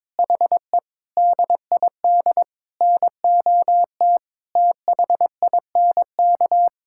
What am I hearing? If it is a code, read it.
Morse code, heard as a keyed tone, about 22 wpm, 700 Hz. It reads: HE DID NOT THINK
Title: Sentences from Top 1600 Words 22wpm